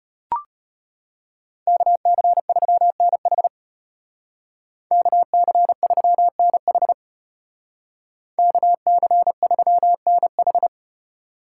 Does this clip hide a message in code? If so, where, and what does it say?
Yes: Morse code from 1.7 to 10.7 s, heard as a keyed tone, about 33 wpm, 700 Hz. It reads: KC3NH KC3NH KC3NH